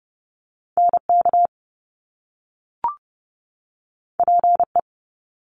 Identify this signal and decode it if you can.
Morse code, heard as a keyed tone, about 30 wpm, 700 Hz. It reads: NK PE
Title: Top Two Letter Combinations in Words 30wpm